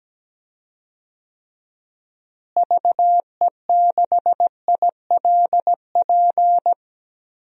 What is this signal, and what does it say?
Morse code, heard as a keyed tone, about 17 wpm, 700 Hz. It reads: VE6ILP
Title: DXCC-100 Call Signs - 6 Characters - Repeated 2x - 17wpm